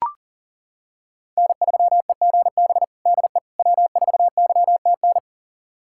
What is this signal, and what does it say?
Morse code, heard as a keyed tone, about 40 wpm, 700 Hz. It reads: N3EGB DE W4YTN
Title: USA Callsign DE Callsign - 5 Characters - 40wpm